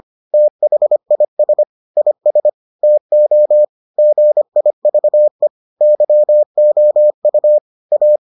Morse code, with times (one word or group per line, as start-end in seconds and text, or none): 0.3-1.6 THIS
2.0-2.5 IS
2.8-3.6 TO
4.0-5.5 GIVE
5.8-7.6 YOU
7.9-8.2 A